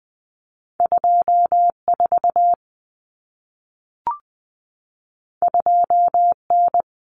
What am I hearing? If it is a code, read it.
Morse code, heard as a keyed tone, about 20 wpm, 700 Hz. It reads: 24 2N
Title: Common Cut Numbers - 2-Digit 20wpm